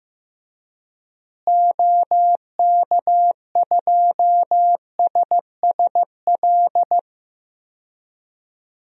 Morse code, heard as a keyed tone, about 15 wpm, 700 Hz. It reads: OK2SSL
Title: DXCC-100 Call Signs - 6 Characters - Repeated 2x - 15wpm